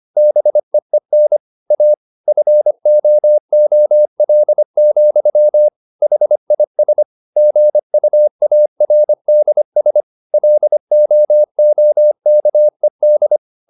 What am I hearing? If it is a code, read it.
Morse code, heard as a keyed tone, about 25 wpm, 600 Hz. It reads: BEEN A FOOL, HIS GUARDS LOOKED